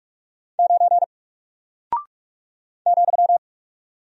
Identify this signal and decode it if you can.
Morse code, heard as a keyed tone, about 44 wpm, 700 Hz. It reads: 9 ,
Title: Single Character (Letter:Number:Punctuation:Prosign) 45wpm